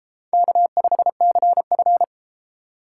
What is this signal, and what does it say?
Morse code, heard as a keyed tone, about 31 wpm, 700 Hz. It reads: K5CF